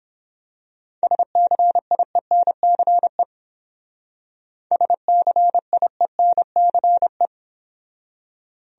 Morse code, heard as a keyed tone, about 26 wpm, 700 Hz. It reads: SCIENCE SCIENCE
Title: Speed-Racing Top 500 Words 22wpm